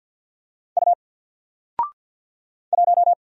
Morse code, heard as a keyed tone, about 50 wpm, 700 Hz. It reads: U 1
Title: Single Letter-Number 50wpm